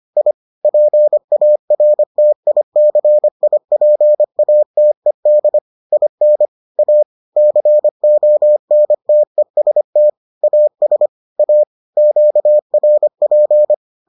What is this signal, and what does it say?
Morse code, heard as a keyed tone, about 25 wpm, 600 Hz. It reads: I PARTICIPATED IN A CONTEST AS A QRP